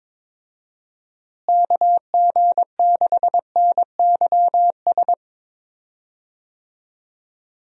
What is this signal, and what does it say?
Morse code, heard as a keyed tone, about 22 wpm, 700 Hz. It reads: KG6NYS
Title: USA Call Signs - 6 Characters - Repeated 3x - 22wpm